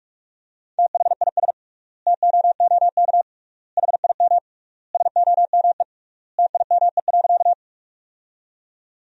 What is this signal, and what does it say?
Morse code, heard as a keyed tone, about 41 wpm, 700 Hz. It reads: THIS TOOK HIM SOME TIME.
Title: 3x Word-Spacing - Sentences from Top 300 Words - 45wpm